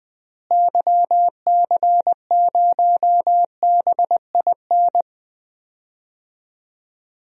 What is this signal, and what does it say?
Morse code, heard as a keyed tone, about 20 wpm, 700 Hz. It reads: YC0BIN
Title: DXCC-100 Call Signs - 6 Characters - Repeated 3x - 20wpm